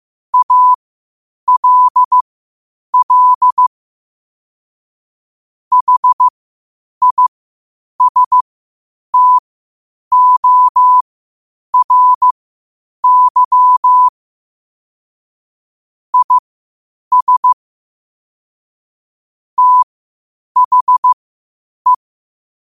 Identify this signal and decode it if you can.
Morse code, heard as a keyed tone, about 15 wpm, spaced as for 8 wpm, 1000 Hz. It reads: ALL HISTORY IS THE
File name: qotdfiles / 8wpm